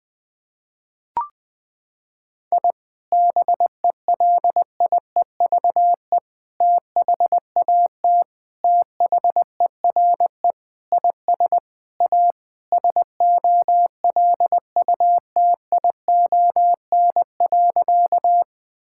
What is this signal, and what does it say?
Morse code, heard as a keyed tone, about 20 wpm, 700 Hz. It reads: I BELIEVE THAT THERE IS A SOLUTION.
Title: Sentences with 4-Word Phrases - 20wpm